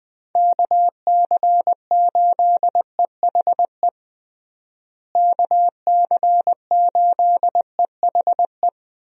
Morse code, heard as a keyed tone, about 20 wpm, 700 Hz. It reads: KC8EHE KC8EHE
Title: USA Call Signs - 6 Characters - Repeated 3x - 20wpm